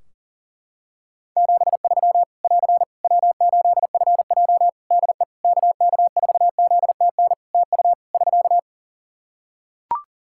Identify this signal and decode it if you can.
Morse code, heard as a keyed tone, about 40 wpm, 700 Hz. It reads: 73 <AR> W8FJ DE KK4ZTN TU <SK>
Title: Practice QSO - Exchange 4 - Ending - 40wpm